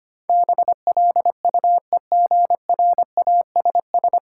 Morse code, heard as a keyed tone, about 25 wpm, 700 Hz. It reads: BLUEGRASS
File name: practice / 25wpm / Difficult Words - Speed-Racing with Hints - 25wpm